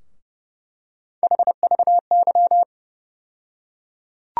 Morse code, heard as a keyed tone, about 30 wpm, 700 Hz. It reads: HVY
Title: USA Call Sign Postfixes - 3 Letters 30wpm